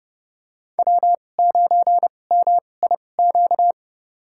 Morse code, heard as a keyed tone, about 30 wpm, spaced as for 22 wpm, 700 Hz. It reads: W9MIQ
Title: USA Call Signs - 5 Characters - 2x Character Spacing - 30wpm